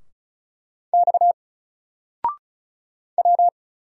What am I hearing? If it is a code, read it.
Morse code, heard as a keyed tone, about 36 wpm, 700 Hz. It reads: X W
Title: Single Character (Letter:Number:Punctuation:Prosign) Alt 35wpm